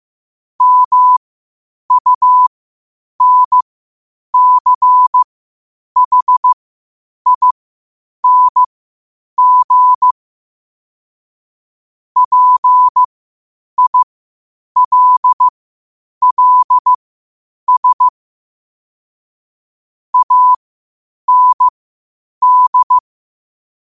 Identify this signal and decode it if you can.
Morse code, heard as a keyed tone, about 15 wpm, spaced as for 8 wpm, 1000 Hz. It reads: MUNCHING PILLS AND
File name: qotdfiles / 8wpm